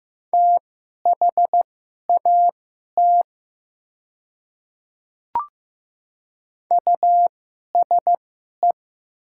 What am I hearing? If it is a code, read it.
Morse code, heard as a keyed tone, about 15 wpm, spaced as for 11 wpm, 700 Hz. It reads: THAT USE
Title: 2x Character Spacing - Top 100 Words - 15wpm